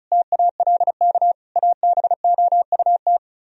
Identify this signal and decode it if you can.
Morse code, heard as a keyed tone, about 45 wpm, 700 Hz. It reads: TALK ABOUT